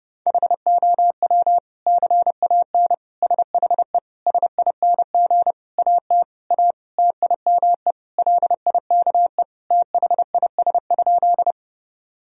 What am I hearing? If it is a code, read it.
Morse code, heard as a keyed tone, about 32 wpm, 700 Hz. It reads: HOW CAN SHE SING AT A TIME LIKE THIS?
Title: Sentences from Top 1000 Words - Review Part 2 - 30wpm